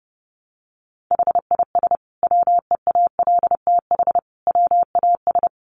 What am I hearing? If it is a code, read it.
Morse code, heard as a keyed tone, about 30 wpm, 700 Hz. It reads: HIS WEALTH WAS
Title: Sentences from Top 1200 Words 30wpm